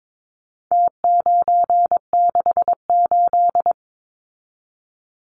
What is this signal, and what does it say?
Morse code, heard as a keyed tone, about 22 wpm, 700 Hz. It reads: T968
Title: Numbers - Contest Style - 4-Digit 22wpm